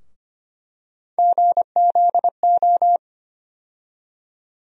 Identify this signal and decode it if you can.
Morse code, heard as a keyed tone, about 25 wpm, 700 Hz. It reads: GZO
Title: USA Call Sign Postfixes - 3 Letters 25wpm